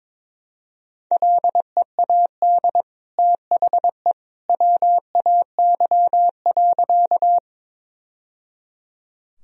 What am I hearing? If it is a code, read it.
Morse code, heard as a keyed tone, about 21 wpm, 700 Hz. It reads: LEAD THE WAY.
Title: Sentences from Top 1000 Words - Review Part 2 - 22wpm